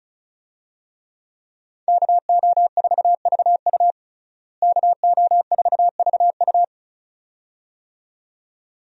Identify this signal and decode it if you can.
Morse code, heard as a keyed tone, about 34 wpm, 700 Hz. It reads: KO4VU KO4VU
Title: USA Call Signs - 5 Characters - Repeated 2x - 35wpm